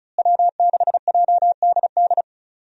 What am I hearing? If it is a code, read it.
Morse code, heard as a keyed tone, about 36 wpm, 700 Hz. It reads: W6JDD
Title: USA Call Signs - 5 Characters - Repeated 2x - 35wpm